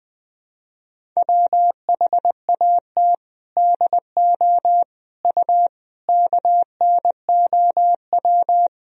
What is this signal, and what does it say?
Morse code, heard as a keyed tone, about 21 wpm, 700 Hz. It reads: WHAT DO U KNOW
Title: Sentences from Top 100 Words and Abbreviations 20wpm